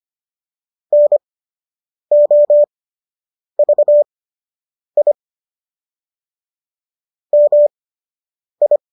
Morse code, heard as a keyed tone, about 25 wpm, spaced as for 8 wpm, 590 Hz. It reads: NOVI MI